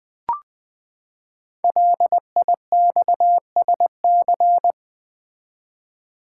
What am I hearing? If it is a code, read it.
Morse code, heard as a keyed tone, about 20 wpm, 700 Hz. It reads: LIXSC
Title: Code Groups - 5 Letters - 20wpm